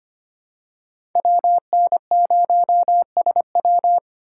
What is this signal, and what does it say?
Morse code, heard as a keyed tone, about 25 wpm, 700 Hz. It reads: WN0SW
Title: USA Call Signs - 5 Characters - 3x Character Spacing - 25wpm